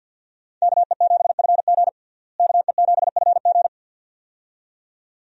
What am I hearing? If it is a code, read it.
Morse code, heard as a keyed tone, about 51 wpm, 700 Hz. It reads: KE7FG KE7FG
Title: USA Call Signs - 5 Characters - Repeated 2x - 50wpm